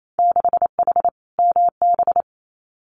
Morse code, heard as a keyed tone, about 28 wpm, 700 Hz. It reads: 6H MB
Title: Winter Field Day Exchanges Encore 28wpm